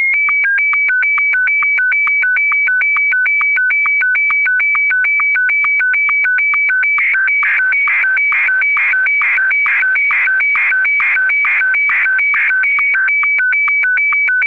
SSTV (slow-scan television), Martin 1.